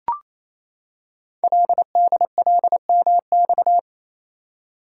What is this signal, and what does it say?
Morse code, heard as a keyed tone, about 29 wpm, 700 Hz. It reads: LDLMX
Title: Code Groups - 5 Letters - 28wpm